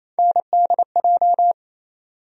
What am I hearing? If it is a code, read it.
Morse code, heard as a keyed tone, about 28 wpm, 700 Hz. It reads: NDJ
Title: USA Call Sign Postfixes - 3 Letters 28wpm